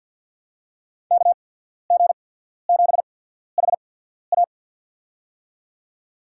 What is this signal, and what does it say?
Morse code, heard as a keyed tone, about 53 wpm, spaced as for 14 wpm, 700 Hz. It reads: KG7HA
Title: USA Call Signs - 5 Characters - 8x Character Spacing - 50wpm